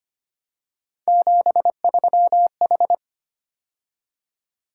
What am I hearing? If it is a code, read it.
Morse code, heard as a keyed tone, about 25 wpm, 700 Hz. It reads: 73H
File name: Three Letter-Number Combinations - 25wpm